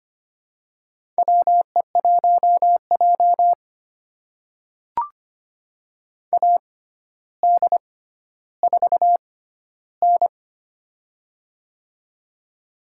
Morse code, heard as a keyed tone, about 25 wpm, spaced as for 13 wpm, 700 Hz. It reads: WE1J AD4N
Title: USA Call Signs - 4 Characters - 6x Character Spacing - 25wpm